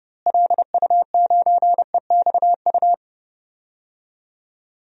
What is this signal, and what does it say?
Morse code, heard as a keyed tone, about 30 wpm, 700 Hz. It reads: LU9EXU
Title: DXCC-100 Call Signs - 6 Characters - Repeated 3x - 30wpm